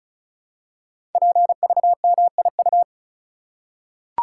Morse code, heard as a keyed tone, about 36 wpm, 700 Hz. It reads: PVMIU
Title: Code Groups - 5 Letter:Numbers - 35wpm